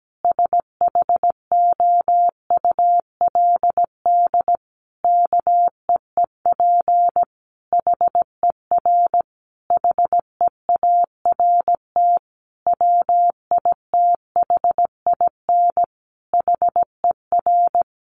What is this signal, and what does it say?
Morse code, heard as a keyed tone, about 17 wpm, 700 Hz. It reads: SHOULD KEEP HER HEART WITHIN HER
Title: Sentences from Top 1200 Words 17wpm